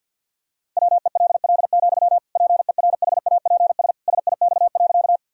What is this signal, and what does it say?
Morse code, heard as a keyed tone, about 51 wpm, 700 Hz. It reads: WELL, PERHAPS SIX.
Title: Sentences from Top 1000 Words - Review Part 2 - 50wpm